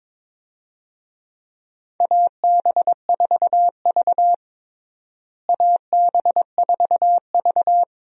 Morse code, heard as a keyed tone, about 22 wpm, 700 Hz. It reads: AB4V AB4V
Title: USA Call Signs - 4 Characters - Repeated 2x - 22wpm